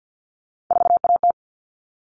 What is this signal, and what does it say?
Morse code, heard as a keyed tone, about 48 wpm, 700 Hz. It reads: 4AT
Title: All Cut Numbers - 3-Digit 50wpm